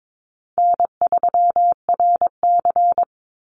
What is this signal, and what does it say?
Morse code, heard as a keyed tone, about 22 wpm, 700 Hz. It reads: N3RC
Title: Easy Call Signs Repeated 2x - 22wpm